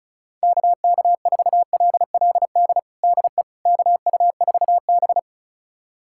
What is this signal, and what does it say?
Morse code, heard as a keyed tone, about 36 wpm, 700 Hz. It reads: KK4LLD DE KU4B
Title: USA Callsign DE Callsign 35wpm